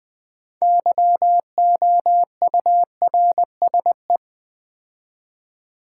Morse code, heard as a keyed tone, about 20 wpm, 700 Hz. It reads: YOURSE
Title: Build A Word - 2 Letters At A Time - 8-Letter Words - 20wpm